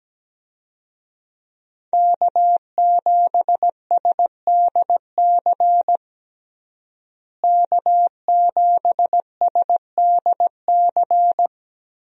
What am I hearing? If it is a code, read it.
Morse code, heard as a keyed tone, about 17 wpm, 700 Hz. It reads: K7SDC K7SDC
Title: USA Call Signs - 5 Characters - Repeated 3x - 17wpm